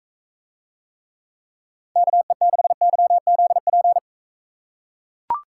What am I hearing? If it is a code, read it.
Morse code, heard as a keyed tone, about 41 wpm, 700 Hz. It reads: KE6YZP